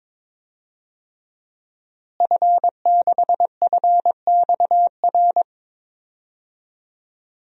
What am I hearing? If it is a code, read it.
Morse code, heard as a keyed tone, about 22 wpm, 700 Hz. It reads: F6FXR
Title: DXCC-100 Call Signs - 5 Characters - Repeated 2x - 22wpm